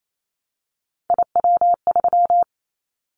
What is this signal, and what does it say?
Morse code, heard as a keyed tone, about 28 wpm, 700 Hz. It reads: IW3
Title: Callsign Prefixes for Top 25 DXCC - 28wpm